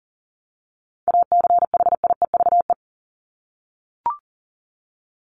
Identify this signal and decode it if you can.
Morse code, heard as a keyed tone, about 40 wpm, 700 Hz. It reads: ACHIEVE
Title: Top 751-1000 Words 40wpm